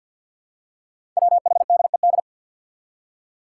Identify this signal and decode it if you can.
Morse code, heard as a keyed tone, about 50 wpm, 700 Hz. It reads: WHDED
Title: Five Challenging Letter-Number Combinations 50wpm